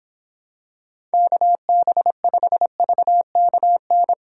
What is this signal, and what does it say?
Morse code, heard as a keyed tone, about 26 wpm, 700 Hz. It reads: KB5VKN